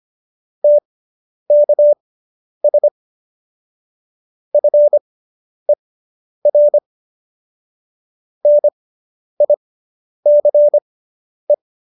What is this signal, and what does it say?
Morse code, heard as a keyed tone, about 25 wpm, spaced as for 10 wpm, 590 Hz. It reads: TKS FER NICE